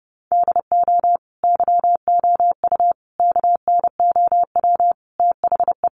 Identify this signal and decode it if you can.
Morse code, heard as a keyed tone, about 28 wpm, 700 Hz. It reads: DO YOU KNOW THE
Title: Sentences from Top 1400 Words 30wpm